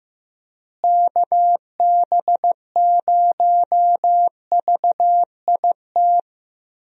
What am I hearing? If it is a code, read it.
Morse code, heard as a keyed tone, about 15 wpm, 700 Hz. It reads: KB0VIT
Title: USA Call Signs - 6 Characters - Repeated 3x - 15wpm